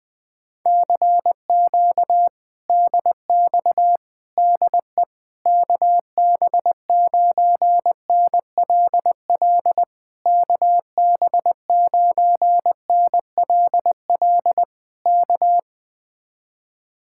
Morse code, heard as a keyed tone, about 20 wpm, 700 Hz. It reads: CQ DX DE KB9NLL KB9NLL K
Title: Speed-Racing Common QSO Phrases - 17wpm